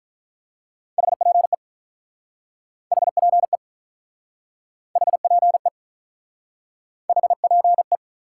Morse code, heard as a keyed tone, about 44 wpm, 700 Hz. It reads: HPE HPE HPE HPE
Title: Speed-Racing Top 180 QSO Elements 35wpm